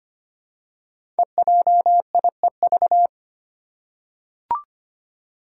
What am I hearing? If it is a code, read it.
Morse code, heard as a keyed tone, about 25 wpm, 700 Hz. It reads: EJIEV
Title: Five Challenging Letter-Number Combinations 25wpm